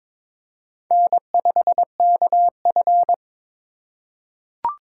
Morse code, heard as a keyed tone, about 22 wpm, 700 Hz. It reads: N5KF
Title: USA Call Signs - 4 Characters - 3x Character Spacing - 22wpm